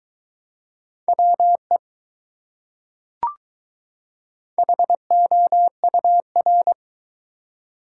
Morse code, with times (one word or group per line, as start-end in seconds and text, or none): 1.1-1.8 WE
4.6-6.7 HOUR